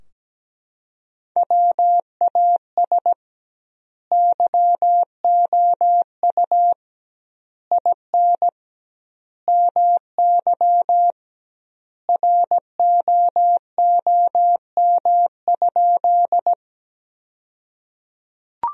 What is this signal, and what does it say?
Morse code, heard as a keyed tone, about 17 wpm, 700 Hz. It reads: WAS YOU IN MY ROOM?
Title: 2x Word-Spacing - Sentences from Top 300 Words - 17wpm